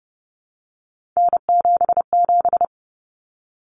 Morse code, heard as a keyed tone, about 30 wpm, 700 Hz. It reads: N77
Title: Common Cut Numbers - 3-Digit 30wpm